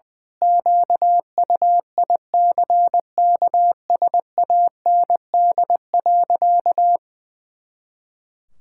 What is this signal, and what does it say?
Morse code, heard as a keyed tone, about 20 wpm, 700 Hz. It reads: QUICKSAND.
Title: Pangram Challenge v2 - 20wpm